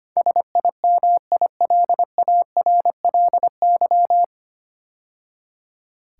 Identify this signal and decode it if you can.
Morse code, heard as a keyed tone, about 25 wpm, 700 Hz. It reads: SIMILARLY
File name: practice / 25wpm / 9 Letter Words - First 3 Letters Given - 25wpm